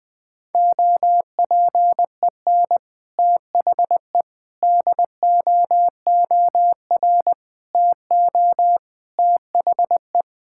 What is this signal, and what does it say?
Morse code, heard as a keyed tone, about 20 wpm, 700 Hz. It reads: OPEN THE DOOR TO THE
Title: Sentences from Top 1000 Words - Review Part 6 - 20wpm